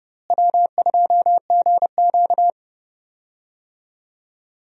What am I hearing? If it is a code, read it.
Morse code, heard as a keyed tone, about 30 wpm, 700 Hz. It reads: W2GQ
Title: USA Call Signs - 4 Characters - Repeated 3x - 30wpm